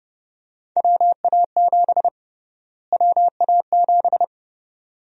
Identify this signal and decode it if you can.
Morse code, heard as a keyed tone, about 30 wpm, 700 Hz. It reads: WA7 WA7
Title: USA Call Sign Prefixes - Repeated 2x - 30wpm